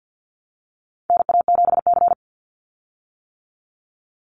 Morse code, heard as a keyed tone, about 50 wpm, 700 Hz. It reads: NA7C
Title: Call Signs - 50wpm